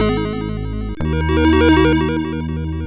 There is no code.